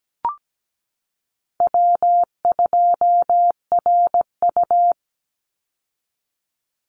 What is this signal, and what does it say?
Morse code, heard as a keyed tone, about 17 wpm, 700 Hz. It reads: W2RU
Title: Easy Call Signs Repeated 3x - 17wpm